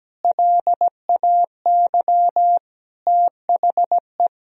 Morse code, heard as a keyed tone, about 17 wpm, 700 Hz. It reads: LAY THE
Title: Sentences from Top 1000 Words - Review Part 2 - 17wpm